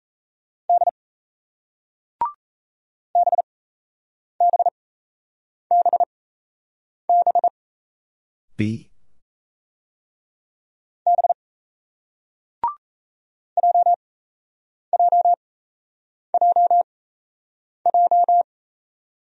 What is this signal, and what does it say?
Morse code, heard as a keyed tone, about 36 wpm, spaced as for 8 wpm, 700 Hz. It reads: D BBBB B JJJJ